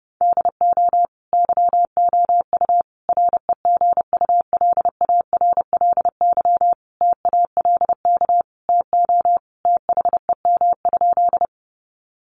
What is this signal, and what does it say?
Morse code, heard as a keyed tone, about 30 wpm, 700 Hz. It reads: DO YOU REGULARLY TALK TO THEM?
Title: Sentences from Top 1400 Words 30wpm